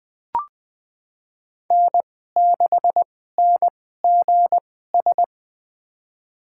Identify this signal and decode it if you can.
Morse code, heard as a keyed tone, about 20 wpm, spaced as for 14 wpm, 700 Hz. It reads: N6NGS
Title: USA Call Signs - 5 Characters - 2x Character Spacing - 20wpm